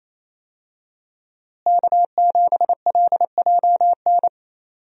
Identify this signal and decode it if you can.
Morse code, heard as a keyed tone, about 27 wpm, 700 Hz. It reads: K7LJN
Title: USA Call Signs - 5 Characters - Repeated 2x - 28wpm